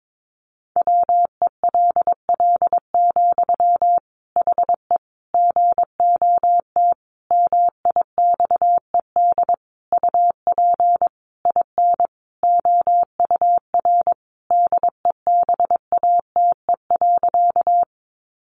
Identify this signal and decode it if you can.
Morse code, heard as a keyed tone, about 22 wpm, 700 Hz. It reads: WELL, HE GOT MIXED UP IN OUR DEBATE.
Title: Sentences from Top 1500 Words 22wpm